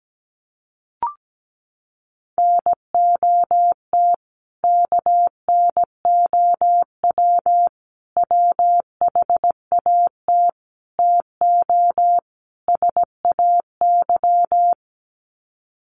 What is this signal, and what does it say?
Morse code, heard as a keyed tone, about 17 wpm, 700 Hz. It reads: NOT KNOW WHAT TO SAY
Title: Sets of 5 Words 17wpm